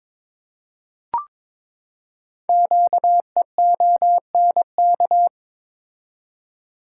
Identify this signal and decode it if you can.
Morse code, heard as a keyed tone, about 22 wpm, 700 Hz. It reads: QEONK
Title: Code Groups - 5 Letters - 22wpm